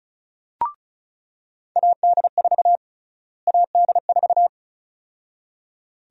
Morse code, heard as a keyed tone, about 34 wpm, 700 Hz. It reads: AD4 AD4
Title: USA Call Sign Prefixes - Repeated 2x - 35wpm